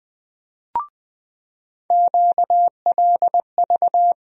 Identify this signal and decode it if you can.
Morse code, heard as a keyed tone, about 20 wpm, 700 Hz. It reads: QLV
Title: USA Call Sign Postfixes - 3 Letters 20wpm